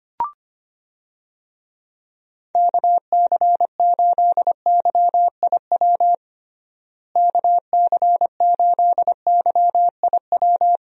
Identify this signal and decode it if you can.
Morse code, heard as a keyed tone, about 25 wpm, 700 Hz. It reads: KC8YIW KC8YIW
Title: USA Call Signs - 6 Characters - Repeated 3x - 25wpm